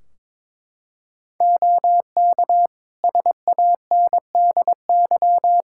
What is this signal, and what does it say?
Morse code, heard as a keyed tone, about 22 wpm, 700 Hz. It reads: OK SANDY
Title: K1USN SST QSO Practice 22wpm